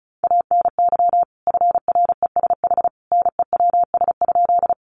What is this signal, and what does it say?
Morse code, heard as a keyed tone, about 34 wpm, 700 Hz. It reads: ANY FRESH NEWS?
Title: Sentences from Top 1200 Words 35wpm